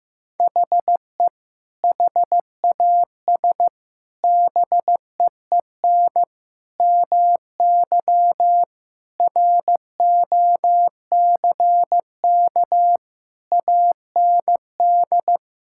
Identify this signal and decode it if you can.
Morse code, heard as a keyed tone, about 14 wpm, 700 Hz. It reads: HE HAS BEEN MY ROCK AND
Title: Sentences from Top 1000 Words - Review Part 5 - 15wpm